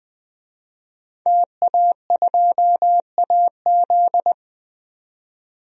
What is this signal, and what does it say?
Morse code, heard as a keyed tone, about 20 wpm, 700 Hz. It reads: TA2AZ
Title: DXCC-100 Call Signs - 5 Characters - 20wpm